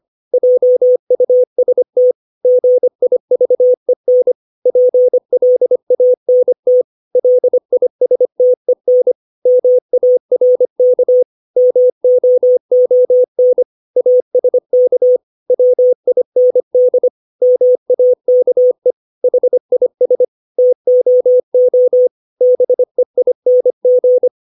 Morse code, heard as a keyed tone, about 25 wpm, 500 Hz. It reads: JUST GIVEN PLANT LISTEN MARK MOON ASK WIND MAKE HIS TOO BEING
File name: 632 words / 632 Words 25WPM